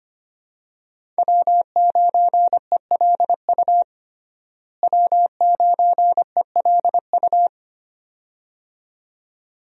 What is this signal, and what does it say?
Morse code, heard as a keyed tone, about 25 wpm, 700 Hz. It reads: W9ELU W9ELU
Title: USA Call Signs - 5 Characters - Repeated 2x - 25wpm